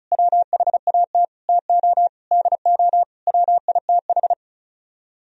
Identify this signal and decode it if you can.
Morse code, heard as a keyed tone, about 45 wpm, 700 Hz. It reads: WHAT TO DO WITH